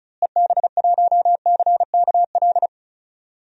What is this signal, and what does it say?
Morse code, heard as a keyed tone, about 36 wpm, 700 Hz. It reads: EB1CKL
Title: DXCC-100 Call Signs - 6 Characters - Repeated 2x - 35wpm